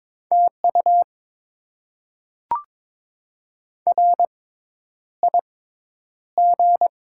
Morse code, heard as a keyed tone, about 22 wpm, spaced as for 10 wpm, 690 Hz. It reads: TU RIG
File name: Top 20 QSO Elements - 6x Character Spacing - 22wpm